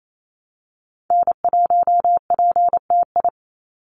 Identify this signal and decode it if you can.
Morse code, heard as a keyed tone, about 28 wpm, 700 Hz. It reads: N1PTI
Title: USA Call Signs 28wpm